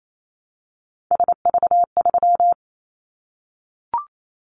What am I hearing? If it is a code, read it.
Morse code, heard as a keyed tone, about 28 wpm, 700 Hz. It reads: SV3
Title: Callsign Prefixes for Top 50 DXCC - 28wpm